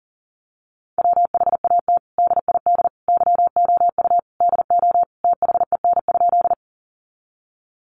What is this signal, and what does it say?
Morse code, heard as a keyed tone, about 40 wpm, 700 Hz. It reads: WHAT DID YOU DO THEN?
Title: Sentences from Top 100 Words 40wpm